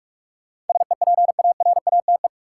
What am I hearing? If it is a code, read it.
Morse code, heard as a keyed tone, about 41 wpm, 700 Hz. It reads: SEPARATE
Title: Sentences from Top 1000 Words - Review Part 7 - New Word Drill - 45wpm